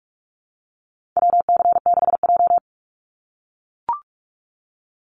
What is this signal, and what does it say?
Morse code, heard as a keyed tone, about 44 wpm, 700 Hz. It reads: WC6J
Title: NA SOTA Activator Callsigns 45wpm